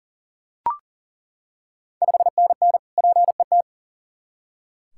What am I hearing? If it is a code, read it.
Morse code, heard as a keyed tone, about 40 wpm, 690 Hz. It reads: 5NN PET
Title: Ontario QSO Party Exchange 40wpm